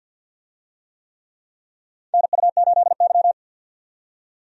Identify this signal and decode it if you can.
Morse code, heard as a keyed tone, about 52 wpm, 700 Hz. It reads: NU8Y
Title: USA Call Signs - 4 Characters - 50wpm